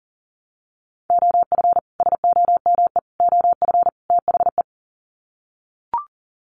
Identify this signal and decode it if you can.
Morse code, heard as a keyed tone, about 40 wpm, 700 Hz. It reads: OF SOME OF THE
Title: Sets of 4 Words 40wpm